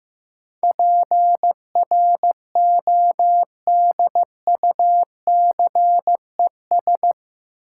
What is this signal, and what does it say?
Morse code, heard as a keyed tone, about 15 wpm, 700 Hz. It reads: PRODUCES
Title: Sentences from Top 1000 Words - Review Part 6 - New Word Drill - 15wpm